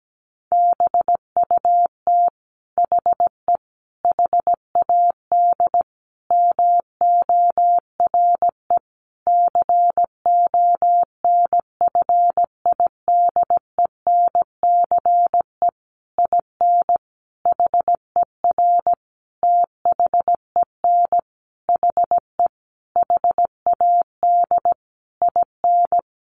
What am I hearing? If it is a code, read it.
Morse code, heard as a keyed tone, about 17 wpm, 700 Hz. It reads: BUT HE HAD MORE CONFIDENCE IN HER THEN HE HAD IN
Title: Sentences from Top 1500 Words 17wpm